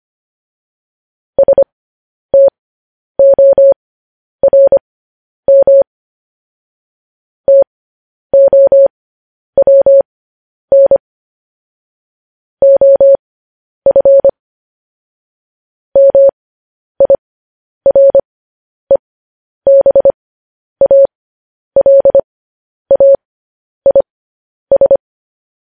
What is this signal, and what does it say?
Morse code, heard as a keyed tone, about 25 wpm, spaced as for 10 wpm, 550 Hz. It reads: STORM TOWN OF MIREBALAIS
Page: News Headlines in Morse Code at 10 WPM